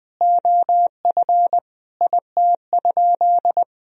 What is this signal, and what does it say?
Morse code, heard as a keyed tone, about 20 wpm, 700 Hz. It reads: OF IT?
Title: Sentences from Top 1000 Words - Review Part 3 - 20wpm